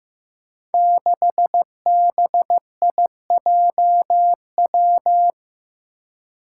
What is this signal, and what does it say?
Morse code, heard as a keyed tone, about 15 wpm, 700 Hz. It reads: 6BIJW
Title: Five Challenging Letter-Number Combinations 15wpm